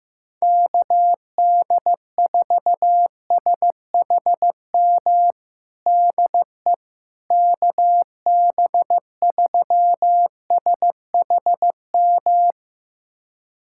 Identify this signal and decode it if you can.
Morse code, heard as a keyed tone, about 15 wpm, 700 Hz. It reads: KD4SHM DE KB3SHM
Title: USA Callsign DE Callsign - 6 Characters - 15wpm